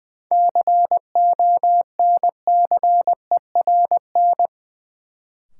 Morse code, heard as a keyed tone, about 20 wpm, 700 Hz. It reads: CONCERN
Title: Sentences from Top 1000 Words - New Word Drill - 20wpm